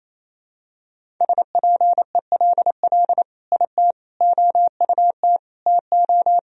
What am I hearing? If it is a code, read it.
Morse code, heard as a keyed tone, about 28 wpm, 700 Hz. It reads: SPELL IT OUT TO
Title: Sentences from Top 1000 Words - Review Part 1 - 28wpm